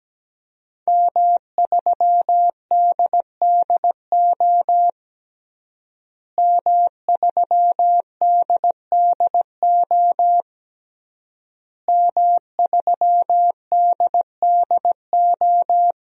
Morse code, heard as a keyed tone, about 17 wpm, 700 Hz. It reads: M3DDO M3DDO M3DDO
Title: DXCC-100 Call Signs - 5 Characters - Repeated 3x - 17wpm